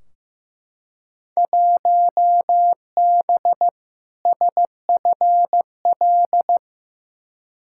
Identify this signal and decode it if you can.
Morse code, heard as a keyed tone, about 15 wpm, 700 Hz. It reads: 1B SFL
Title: Field Day Exchanges Encore - 15wpm